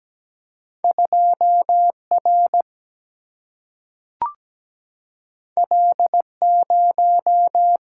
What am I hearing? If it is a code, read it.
Morse code, heard as a keyed tone, about 17 wpm, 700 Hz. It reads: 2 R L 0
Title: Single-Letter Single-Number 17wpm